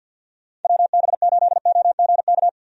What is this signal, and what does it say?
Morse code, heard as a keyed tone, about 50 wpm, 700 Hz. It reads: WB8OGK
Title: USA Call Signs - 6 Characters - Repeated 3x - 50wpm